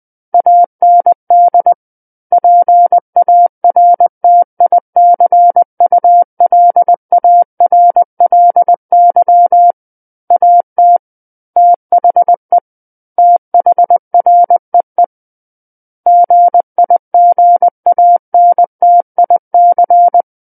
Morse code, heard as a keyed tone, about 20 wpm, 700 Hz. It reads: AND PARTICULARLY AT THE THREE GIGANTIC
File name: 20 WPM